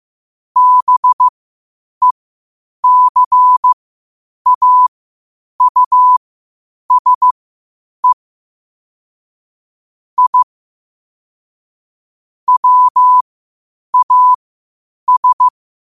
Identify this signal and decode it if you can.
Morse code, heard as a keyed tone, about 15 wpm, spaced as for 8 wpm, 1000 Hz. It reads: BECAUSE I WAS
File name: qotdfiles / 8wpm